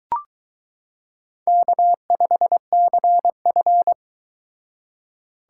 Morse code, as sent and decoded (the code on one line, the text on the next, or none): -.- ..... -.-. ..-.
K5CF